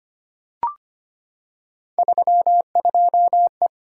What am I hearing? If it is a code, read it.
Morse code, heard as a keyed tone, about 25 wpm, 700 Hz. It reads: 32E
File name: Three Letter-Number Combinations - 25wpm